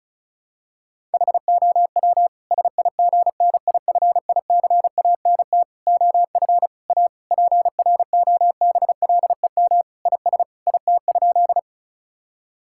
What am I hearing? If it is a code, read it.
Morse code, heard as a keyed tone, about 36 wpm, 700 Hz. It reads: HOW SIGNIFICANT OF A PROBLEM IS IT?
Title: Sentences from Top 1000 Words - Review Part 6 - 35wpm